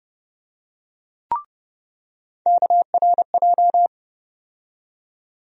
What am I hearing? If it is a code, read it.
Morse code, heard as a keyed tone, about 30 wpm, 700 Hz. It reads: KRJ
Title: USA Call Sign Postfixes - 3 Letters 30wpm